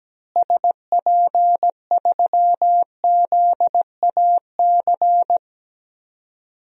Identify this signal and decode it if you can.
Morse code, heard as a keyed tone, about 17 wpm, 700 Hz. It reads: SP3ZAC
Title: DXCC-100 Call Signs - 6 Characters - Repeated 3x - 17wpm